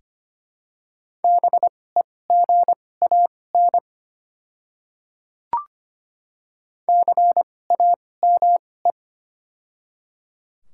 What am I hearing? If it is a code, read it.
Morse code, heard as a keyed tone, about 25 wpm, spaced as for 18 wpm, 700 Hz. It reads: BEGAN CAME
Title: 2x Character Spacing - Top 300 Words - 25wpm